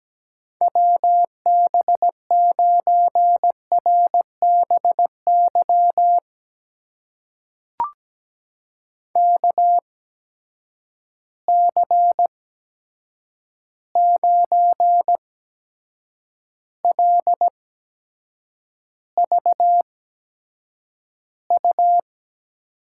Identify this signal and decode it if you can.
Morse code, heard as a keyed tone, about 17 wpm, spaced as for 7 wpm, 700 Hz. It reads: WB9RBY KC9LVU
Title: USA Call Signs - 6 Characters - 8x Character Spacing - 17wpm